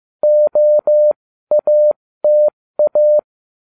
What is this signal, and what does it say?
Morse code, heard as a keyed tone, about 15 wpm, spaced as for 13 wpm, 600 Hz. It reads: OATA